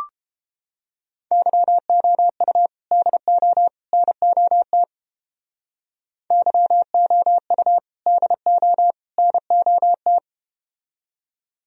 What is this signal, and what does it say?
Morse code, heard as a keyed tone, about 31 wpm, 700 Hz. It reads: YOU DO NOT YOU DO NOT